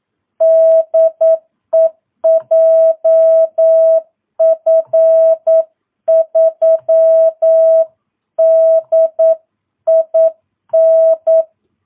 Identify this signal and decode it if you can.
Morse code, heard as a keyed tone, about 9 wpm, 650 Hz. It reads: DE JF3DIN
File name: jh8cbh / 2022cw